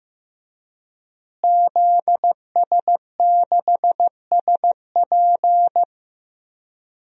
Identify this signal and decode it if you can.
Morse code, heard as a keyed tone, about 15 wpm, 700 Hz. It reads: ZS6SP
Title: DXCC-100 Call Signs - 5 Characters - Repeated 3x - 15wpm